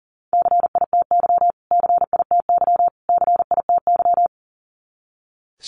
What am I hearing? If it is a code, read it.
Morse code, heard as a keyed tone, about 37 wpm, 700 Hz. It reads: CITY CITY CITY
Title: Top 300 Words Repeated 40wpm